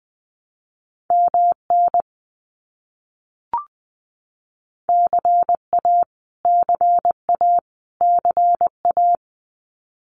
Morse code, heard as a keyed tone, about 20 wpm, 700 Hz. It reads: MN CA CA CA
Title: US States Repeated 3x 20wpm